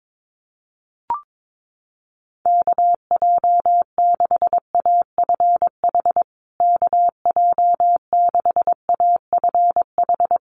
Morse code, heard as a keyed tone, about 22 wpm, 690 Hz. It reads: KJ6AFH KJ6AFH
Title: USA Call Signs Repeated 3x 22wpm